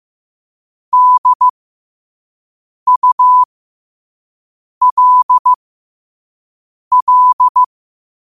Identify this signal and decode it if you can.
Morse code, heard as a keyed tone, about 15 wpm, spaced as for 5 wpm, 1000 Hz. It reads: DULL
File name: qotdfiles / 5wpm